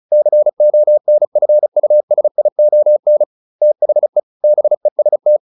Morse code, heard as a keyed tone, about 34 wpm, 600 Hz. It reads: CONFUSION THE BEST